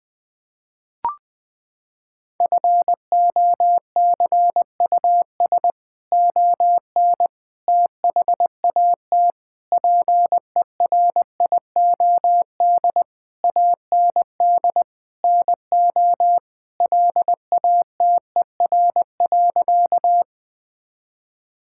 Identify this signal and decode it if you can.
Morse code, heard as a keyed tone, about 20 wpm, 700 Hz. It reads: FOCUS ON THAT PERIOD AND NO LATER.
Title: Sentences from Top 1000 Words - Review Part 5 - 20wpm